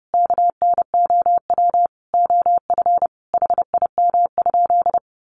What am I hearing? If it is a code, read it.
Morse code, heard as a keyed tone, about 30 wpm, 700 Hz. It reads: KNOW OF HIM?
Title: Sentences from Top 100 Words 30wpm